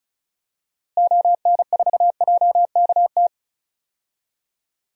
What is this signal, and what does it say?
Morse code, heard as a keyed tone, about 36 wpm, 700 Hz. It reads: ON4JKT
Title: DXCC-100 Call Signs - 6 Characters - Repeated 2x - 35wpm